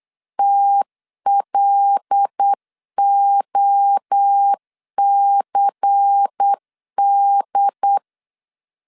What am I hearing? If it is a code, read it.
Morse code, heard as a keyed tone, about 8 wpm, 800 Hz. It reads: TLOCD
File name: LEZIONI / 6 PKX0 / 6 PKX 8wpm 3041